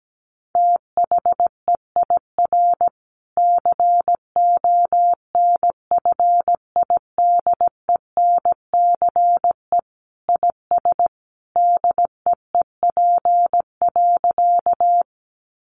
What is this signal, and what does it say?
Morse code, heard as a keyed tone, about 17 wpm, 700 Hz. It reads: THEIR CONFIDENCE IS DEEP.
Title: Sentences from Top 1500 Words 17wpm